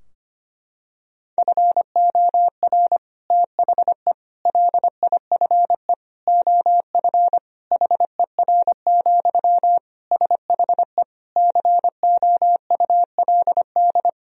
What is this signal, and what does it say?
Morse code, heard as a keyed tone, about 25 wpm, 700 Hz. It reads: FOR THE LIFE OF HER, SHE COULD
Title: Sentences from Top 1000 Words - Review Part 2 - 25wpm